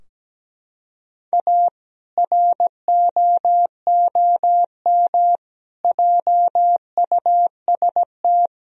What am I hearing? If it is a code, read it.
Morse code, heard as a keyed tone, about 17 wpm, 700 Hz. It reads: A ROOM JUST
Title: Sentences from Top 1000 Words - Review Part 2 - 17wpm